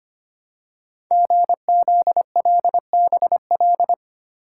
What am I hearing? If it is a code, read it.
Morse code, heard as a keyed tone, about 25 wpm, 700 Hz. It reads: GZLBL
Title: Code Groups - 5 Letters - 25wpm